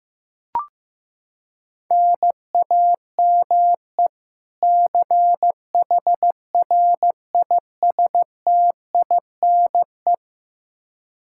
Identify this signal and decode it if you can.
Morse code, heard as a keyed tone, about 15 wpm, 700 Hz. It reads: NAME CHRISTINE
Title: Practice QSO - Exchange 1 - Beginning - 15wpm